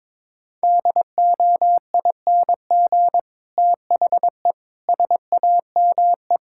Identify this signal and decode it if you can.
Morse code, heard as a keyed tone, about 21 wpm, 700 Hz. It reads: DOING THE SAME
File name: practice / 22wpm / Sets of 7 Words - 22wpm